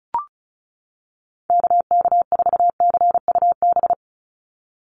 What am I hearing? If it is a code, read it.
Morse code, heard as a keyed tone, about 34 wpm, 700 Hz. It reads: KK4CUB
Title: USA Call Signs 35wpm